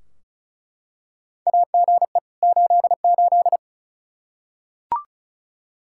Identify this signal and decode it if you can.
Morse code, heard as a keyed tone, about 34 wpm, 700 Hz. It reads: AGE 88
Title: Practice QSO - Exchange 3 - More Info - 35wpm